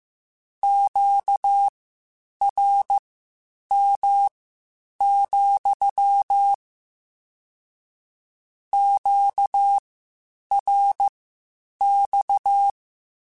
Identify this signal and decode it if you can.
Morse code, heard as a keyed tone, about 15 wpm, spaced as for 8 wpm, 800 Hz. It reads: QRM, QRX